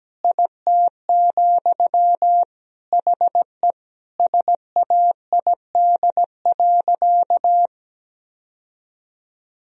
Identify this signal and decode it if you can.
Morse code, heard as a keyed tone, about 17 wpm, 700 Hz. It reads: IT, HE SAID.
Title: Sentences from Top 1000 Words - Review Part 3 - 17wpm